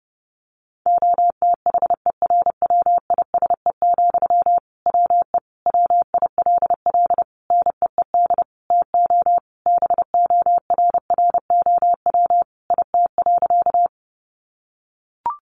Morse code, heard as a keyed tone, about 30 wpm, 700 Hz. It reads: OTHERWISE, WE WILL NEED TO BORROW IT.
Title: Sentences from Top 1500 Words 30wpm